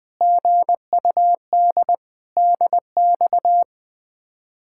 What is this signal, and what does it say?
Morse code, heard as a keyed tone, about 20 wpm, 700 Hz. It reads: GUD DX
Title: Speed-Racing Common QSO Phrases Encore - 20wpm